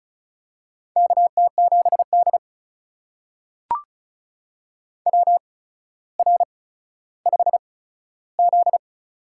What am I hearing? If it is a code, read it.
Morse code, heard as a keyed tone, about 34 wpm, spaced as for 15 wpm, 700 Hz. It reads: KT7D WR5Z
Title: USA Call Signs - 4 Characters - 8x Character Spacing - 35wpm